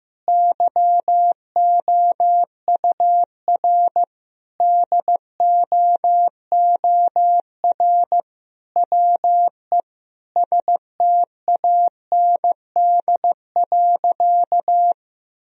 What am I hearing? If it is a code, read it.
Morse code, heard as a keyed tone, about 15 wpm, 700 Hz. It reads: YOUR DOOR WE STAND.
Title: Sentences from Top 1000 Words - Review Part 1 - 15wpm